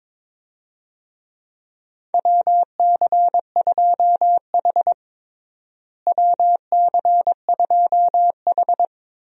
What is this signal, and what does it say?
Morse code, heard as a keyed tone, about 22 wpm, 700 Hz. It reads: WC2H WC2H
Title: USA Call Signs - 4 Characters - Repeated 2x - 22wpm